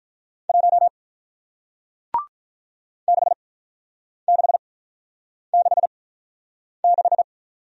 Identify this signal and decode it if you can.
Morse code, heard as a keyed tone, about 47 wpm, spaced as for 9 wpm, 700 Hz. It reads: 1 6666